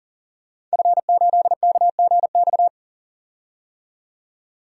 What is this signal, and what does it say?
Morse code, heard as a keyed tone, about 40 wpm, 700 Hz. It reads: F8KGX
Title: DXCC-100 Call Signs - 5 Characters - 40wpm